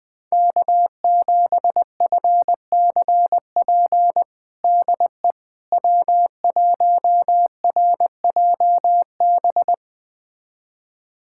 Morse code, heard as a keyed tone, about 20 wpm, 700 Hz. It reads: K7FCP DE W1RJB
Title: USA Callsign DE Callsign - 5 Characters - 20wpm